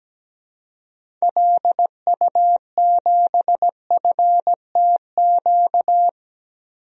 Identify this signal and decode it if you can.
Morse code, heard as a keyed tone, about 17 wpm, 700 Hz. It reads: LU7FTQ
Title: DXCC-100 Call Signs - 6 Characters - 17wpm